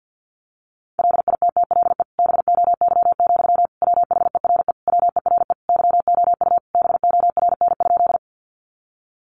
Morse code, heard as a keyed tone, about 49 wpm, 700 Hz. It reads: LITTLE BOY, WHERE WERE YOU BORN?
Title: Sentences from Top 1000 Words 50wpm